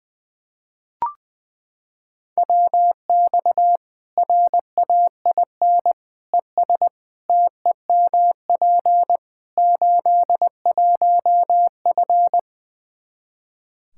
Morse code, heard as a keyed tone, about 20 wpm, 700 Hz. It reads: WX RAIN ES TEMP 81F
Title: Practice QSO - Exchange 2 - Info - 20wpm